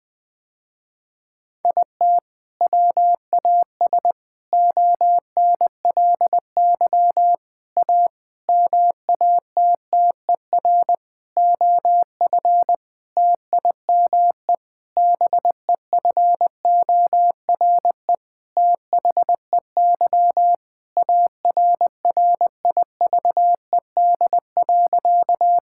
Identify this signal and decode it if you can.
Morse code, heard as a keyed tone, about 20 wpm, 700 Hz. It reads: IT WAS ONLY A MATTER OF TIME BEFORE THEY ARRIVED.
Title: Sentences with 7-Word Phrases - Spoken Beginning - 20wpm